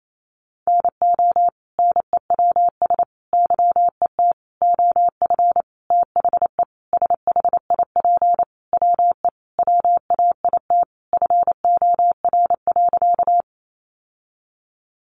Morse code, heard as a keyed tone, about 28 wpm, 700 Hz. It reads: NO NEWS YET OF THE SHIP WE WAIT FOR.
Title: Sentences from Top 800 Words 28wpm